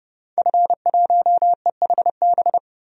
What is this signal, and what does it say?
Morse code, heard as a keyed tone, about 31 wpm, 700 Hz. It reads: F1EHB
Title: DXCC-100 Call Signs - 5 Characters - 30wpm